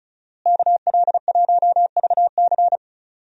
Morse code, heard as a keyed tone, about 36 wpm, 700 Hz. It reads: KL1VC
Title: USA Call Signs - 5 Characters - Repeated 2x - 35wpm